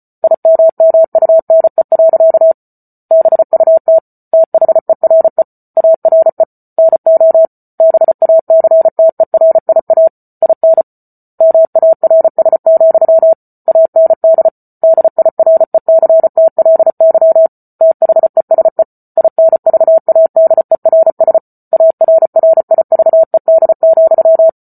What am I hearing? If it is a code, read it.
Morse code, heard as a keyed tone, about 34 wpm, 650 Hz. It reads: IMMUNE. BUT THERE ARE NO BACTERIA IN MARS, AND DIRECTLY THESE INVADERS ARRIVED,
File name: Book Two / 08-Chapter 8 - 34 WPM